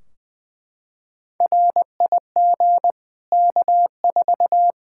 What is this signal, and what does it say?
Morse code, heard as a keyed tone, about 20 wpm, 700 Hz. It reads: RIG K4
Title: Practice QSO - Exchange 2 - Info - 20wpm